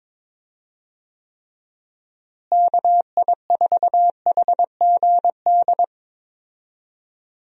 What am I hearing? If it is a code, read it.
Morse code, heard as a keyed tone, about 22 wpm, 700 Hz. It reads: KI4HGD
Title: USA Call Signs - 6 Characters - Repeated 2x - 22wpm